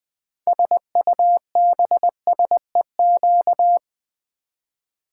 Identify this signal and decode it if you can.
Morse code, heard as a keyed tone, about 20 wpm, 700 Hz. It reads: SUBSEQ
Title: Build A Word - 6 Letters At A Time - 12-Letter Words - 20wpm